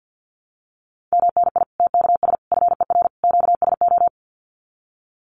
Morse code, heard as a keyed tone, about 50 wpm, 700 Hz. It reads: MNI TKS FER QSO
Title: Common QSO Phrases 50wpm